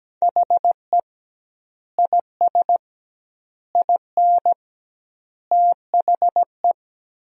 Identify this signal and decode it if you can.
Morse code, heard as a keyed tone, about 17 wpm, 700 Hz. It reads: HE IS IN THE
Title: 2x Word-Spacing - Sentences from Top 500 Words - 17wpm